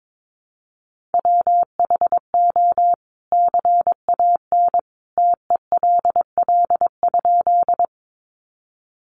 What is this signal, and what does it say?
Morse code, heard as a keyed tone, about 22 wpm, 700 Hz. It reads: WHO CAN TELL?
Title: A Sound Of Thunder 22wpm